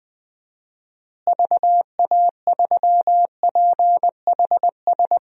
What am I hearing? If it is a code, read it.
Morse code, heard as a keyed tone, about 21 wpm, 700 Hz. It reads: VA3PHS
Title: DXCC-100 Call Signs - 6 Characters - Repeated 3x - 20wpm